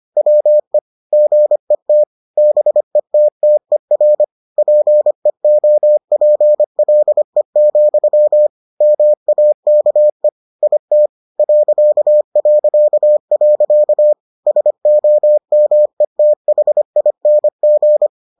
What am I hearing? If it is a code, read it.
Morse code, heard as a keyed tone, about 25 wpm, 600 Hz. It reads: WE GET BETTER PEOPLE, MAKE IT ... SOMETHING